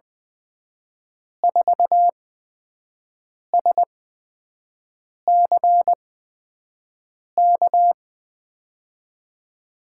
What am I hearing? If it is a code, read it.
Morse code, heard as a keyed tone, about 20 wpm, spaced as for 5 wpm, 700 Hz. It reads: 4SCK
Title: USA Call Signs - 6 Characters - 8x Character Spacing - 20wpm